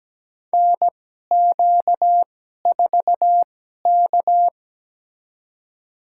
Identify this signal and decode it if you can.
Morse code, heard as a keyed tone, about 17 wpm, spaced as for 12 wpm, 700 Hz. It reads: NQ4K
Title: USA Call Signs - 4 Characters - 2x Character Spacing - 17wpm